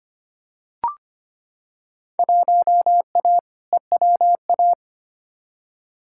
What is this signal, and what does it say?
Morse code, heard as a keyed tone, about 25 wpm, 700 Hz. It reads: 1A EWA
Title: Field Day Exchanges Encore - 25wpm